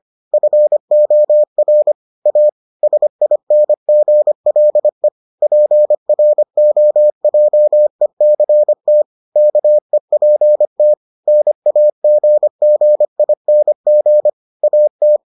Morse code, heard as a keyed tone, about 25 wpm, 600 Hz. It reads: FOR A SINGLE PROJECT KEPT NAGGING AT